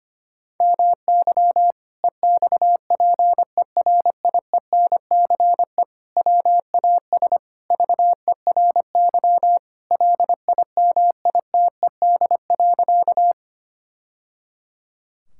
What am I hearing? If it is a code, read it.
Morse code, heard as a keyed tone, about 25 wpm, 700 Hz. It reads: MY EXPERIENCE WAS VERY LIMITED.
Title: Sentences from Top 1000 Words - Review Part 6 - 25wpm